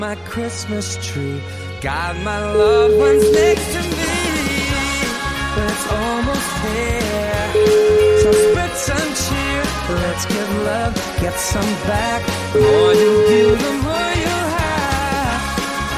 Ringback tone (Europe, 425 Hz), with noise.